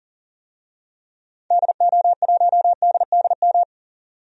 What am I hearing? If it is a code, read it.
Morse code, heard as a keyed tone, about 40 wpm, 700 Hz. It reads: DO1DDM
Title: DXCC-100 Call Signs - 6 Characters - 40wpm